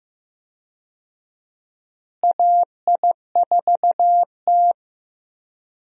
Morse code, heard as a keyed tone, about 16 wpm, 700 Hz. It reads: AI4T
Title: USA Call Signs - 4 Characters - Repeated 2x - 15wpm